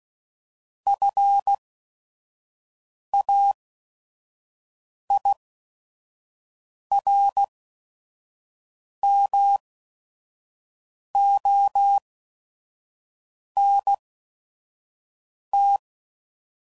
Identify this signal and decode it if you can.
Morse code, heard as a keyed tone, about 16 wpm, spaced as for 5 wpm, 800 Hz. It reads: FAIRMONT
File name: CW Code Groups @5WPM / fromSM's / Cities